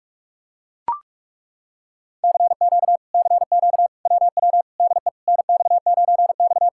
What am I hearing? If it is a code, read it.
Morse code, heard as a keyed tone, about 40 wpm, 700 Hz. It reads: CQ CQ WW DE NX9X